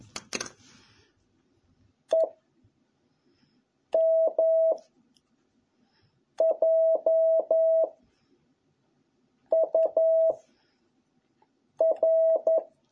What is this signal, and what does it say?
Morse code, heard as a keyed tone, about 11 wpm, 650 Hz. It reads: E M J U R